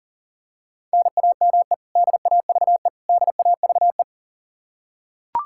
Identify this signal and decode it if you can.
Morse code, heard as a keyed tone, about 40 wpm, 700 Hz. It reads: NAME DAVE DAVE
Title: Practice QSO - Exchange 1 - Beginning - 40wpm